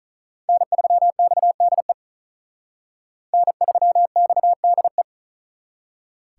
Morse code, heard as a keyed tone, about 46 wpm, 700 Hz. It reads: N3XDE N3XDE